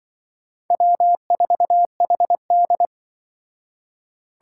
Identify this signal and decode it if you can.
Morse code, heard as a keyed tone, about 25 wpm, 700 Hz. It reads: W4HD
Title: Speed Racing - USA 1x2 Call Signs 20wpm